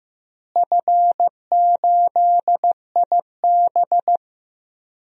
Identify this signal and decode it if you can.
Morse code, heard as a keyed tone, about 15 wpm, 700 Hz. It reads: F8IB
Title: DXCC-100 Call Signs - 4 Characters - Repeated 3x - 15wpm